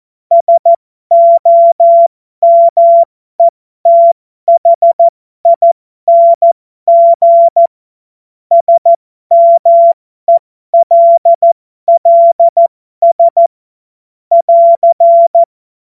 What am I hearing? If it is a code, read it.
Morse code, heard as a keyed tone, about 14 wpm, spaced as for 12 wpm, 680 Hz. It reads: SOMETHING SMELLS <AR>
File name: Morsekurs / Witze